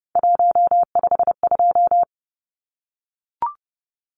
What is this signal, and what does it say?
Morse code, heard as a keyed tone, about 30 wpm, 700 Hz. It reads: 152
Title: Numbers - 3-Digit 30wpm